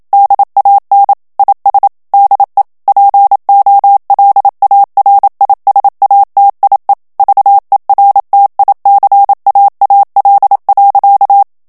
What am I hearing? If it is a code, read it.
Morse code, heard as a keyed tone, about 27 wpm, 800 Hz. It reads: DAN IS DE POLARISATIE VERTICAAL.
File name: HOLLANDAIS texte clair 1 vitesse 26 mots minute